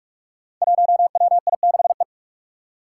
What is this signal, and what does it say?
Morse code, heard as a keyed tone, about 44 wpm, 700 Hz. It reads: 1WI6E
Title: Five Challenging Letter-Number Combinations 45wpm